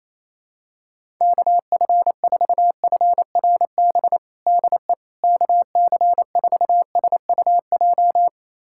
Morse code, heard as a keyed tone, about 28 wpm, 700 Hz. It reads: KF4FRB DE KC4SUJ
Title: USA Callsign DE Callsign - 6 Characters - 28wpm